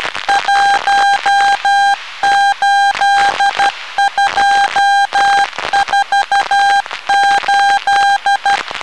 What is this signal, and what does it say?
Morse code, heard as a keyed tone, about 12 wpm, 800 Hz. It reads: 18248